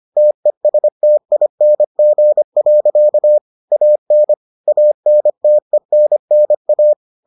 Morse code, heard as a keyed tone, about 25 wpm, 600 Hz. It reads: TESTING. AN ANTENNA